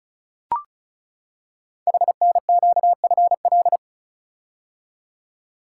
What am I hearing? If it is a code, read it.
Morse code, heard as a keyed tone, about 36 wpm, 690 Hz. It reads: HNQFL
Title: Code Groups - 5 Letters - 35wpm